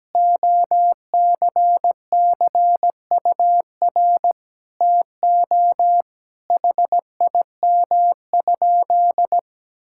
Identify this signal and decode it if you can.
Morse code, heard as a keyed tone, about 17 wpm, 700 Hz. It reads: OCCUR TO HIM?
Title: Sentences from Top 1000 Words - Review Part 6 - 17wpm